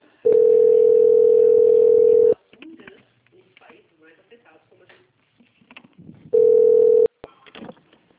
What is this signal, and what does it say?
Ringback tone (North America), 440 + 480 Hz, with speech.
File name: out- - - - - .11443